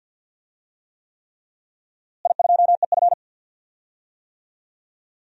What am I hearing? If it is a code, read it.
Morse code, heard as a keyed tone, about 48 wpm, 700 Hz. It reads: I2EF
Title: DXCC-100 Call Signs - 4 Characters - Repeated 3x - 50wpm